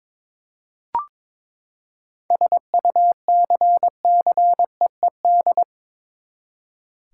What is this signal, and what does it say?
Morse code, heard as a keyed tone, about 22 wpm, 700 Hz. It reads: SUCCEED
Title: Seven Letter Words - 22wpm